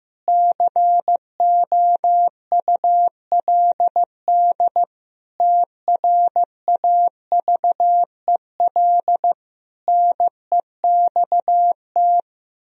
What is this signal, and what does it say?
Morse code, heard as a keyed tone, about 15 wpm, 700 Hz. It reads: COULD TRAVEL NEXT
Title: Sentences from Top 1000 Words - Review Part 2 - 15wpm